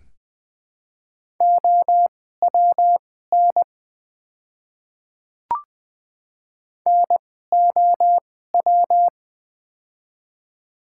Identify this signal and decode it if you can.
Morse code, heard as a keyed tone, about 20 wpm, spaced as for 15 wpm, 700 Hz. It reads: OWN NOW
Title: 2x Character Spacing - Top 300 Words - 20wpm